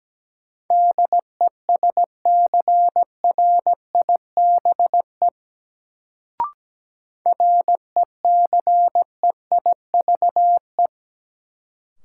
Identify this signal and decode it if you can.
Morse code, heard as a keyed tone, about 17 wpm, 700 Hz. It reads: DESCRIBE RECEIVE
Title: Sentences from Top 1000 Words - New Word Drill - 17wpm